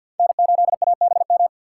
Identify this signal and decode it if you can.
Morse code, heard as a keyed tone, about 53 wpm, 700 Hz. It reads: N8ABM
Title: USA Call Signs - 5 Characters - 6x Character Spacing - 50wpm